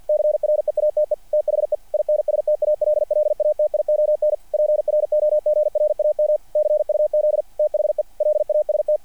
Morse code, heard as a keyed tone, about 48 wpm, 600 Hz, with radio noise. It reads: CREATE THE INSTALLATION PROGRAM CUZ THE LAST
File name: archived / CRSnet